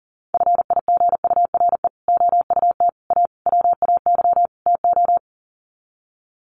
Morse code, heard as a keyed tone, about 40 wpm, 700 Hz. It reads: FIGURE OUT A WAY TO
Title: Sets of 5 Words Encore2 40wpm